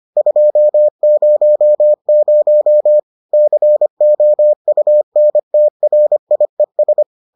Morse code, heard as a keyed tone, about 25 wpm, 600 Hz. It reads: 200 COUNTRIES